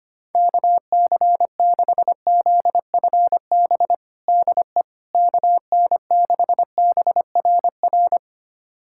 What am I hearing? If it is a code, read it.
Morse code, heard as a keyed tone, about 25 wpm, 700 Hz. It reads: KC6ZFB DE KN6BRR
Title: USA Callsign DE Callsign 25wpm